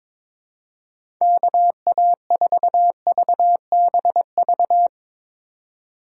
Morse code, heard as a keyed tone, about 23 wpm, 700 Hz. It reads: KA4VBV
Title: USA Call Signs - 6 Characters - 3x Character Spacing - 22wpm